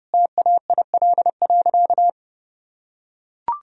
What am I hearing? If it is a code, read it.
Morse code, heard as a keyed tone, about 30 wpm, 700 Hz. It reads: TAIL.
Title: Sentences from Top 1000 Words - Review Part 3 - 30wpm